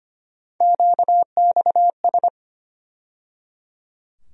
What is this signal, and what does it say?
Morse code, heard as a keyed tone, about 25 wpm, 700 Hz. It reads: QXS
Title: USA Call Sign Postfixes - 3 Letters 25wpm